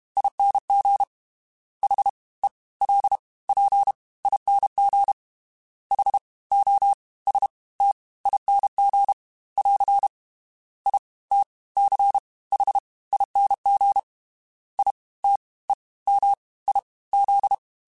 Morse code, heard as a keyed tone, about 32 wpm, 800 Hz. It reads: ING H E L P ING H O S T ING <AR> I T C H ING I T E M I Z
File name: Special INGs / Special INGs-mp3s / Fast ing @30